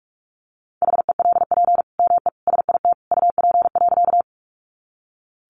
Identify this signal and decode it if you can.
Morse code, heard as a keyed tone, about 44 wpm, 700 Hz. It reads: HELP ME SIT UP.
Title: Sentences from Top 500 Words 45wpm